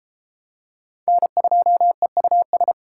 Morse code, heard as a keyed tone, about 32 wpm, 700 Hz. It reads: N2EUS